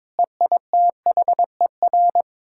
Morse code, heard as a keyed tone, about 22 wpm, 700 Hz. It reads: EITHER
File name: practice / 22wpm / Build A Phrase - 5-Word Phrases - 22wpm